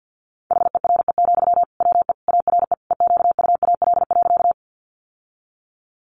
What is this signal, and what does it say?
Morse code, heard as a keyed tone, about 50 wpm, 700 Hz. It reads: HERE, WE ARE EQUAL.
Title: Sentences from Top 1400 Words 50wpm